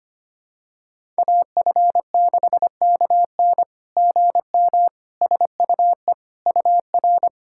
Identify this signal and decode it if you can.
Morse code, heard as a keyed tone, about 25 wpm, 700 Hz. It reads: AF6KN GM SUE UR
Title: SOTA QSO Practice 25wpm